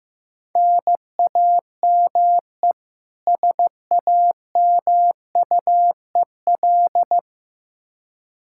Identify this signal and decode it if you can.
Morse code, heard as a keyed tone, about 15 wpm, 700 Hz. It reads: NAME SAMUEL
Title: Practice QSO - Exchange 1 - Beginning - 15wpm